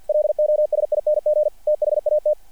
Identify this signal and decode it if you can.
Morse code, heard as a keyed tone, about 48 wpm, 600 Hz. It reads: LOSING THAT
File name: archived / CRSnet